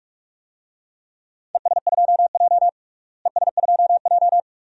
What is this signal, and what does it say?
Morse code, heard as a keyed tone, about 47 wpm, 700 Hz. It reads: ES2J ES2J
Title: DXCC-100 Call Signs - 4 Characters - Repeated 2x - 45wpm